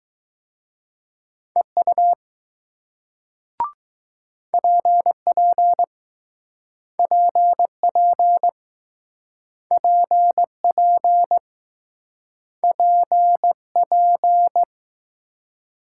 Morse code, heard as a keyed tone, about 19 wpm, 700 Hz. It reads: EU PP PP PP PP